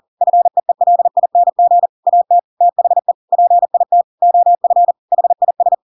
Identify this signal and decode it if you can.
Morse code, heard as a keyed tone, about 40 wpm, 700 Hz. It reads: FEELING AT THE PIT OF HIS
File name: Pinocchio / Pinocchio ch5 40 40 ik1hge